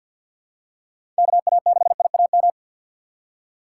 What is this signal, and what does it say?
Morse code, heard as a keyed tone, about 47 wpm, 700 Hz. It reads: KA6IAM
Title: USA Call Signs - 6 Characters - 4x Character Spacing - 50wpm